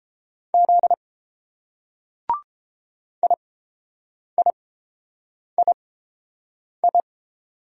Morse code, heard as a keyed tone, about 30 wpm, spaced as for 7 wpm, 700 Hz. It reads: Z IIII